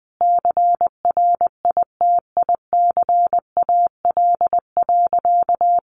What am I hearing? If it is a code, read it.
Morse code, heard as a keyed tone, about 19 wpm, 700 Hz. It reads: CRITICAL.
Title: Sentences from Top 1100 Words 20wpm